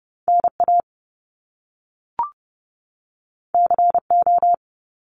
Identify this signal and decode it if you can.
Morse code, heard as a keyed tone, about 30 wpm, 700 Hz. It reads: NA CO
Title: Top Two Letter Combinations in Words 30wpm